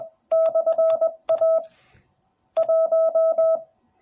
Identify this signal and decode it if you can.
Morse code, heard as a keyed tone, about 20 wpm, 650 Hz. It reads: /A 1